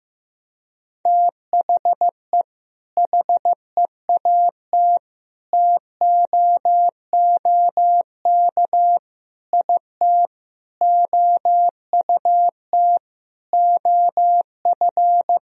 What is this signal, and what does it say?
Morse code, heard as a keyed tone, about 15 wpm, 700 Hz. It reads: THE HEAT TOOK IT OUT OF
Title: Sentences from Top 1000 Words - Review Part 3 - 15wpm